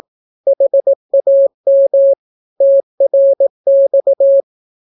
Morse code, heard as a keyed tone, about 18 wpm, 550 Hz. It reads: HAM TRX
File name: naukapodstaw / skroty / skroty